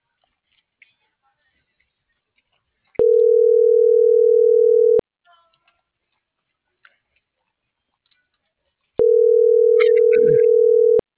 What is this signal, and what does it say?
Ringback tone (North America), 440 + 480 Hz.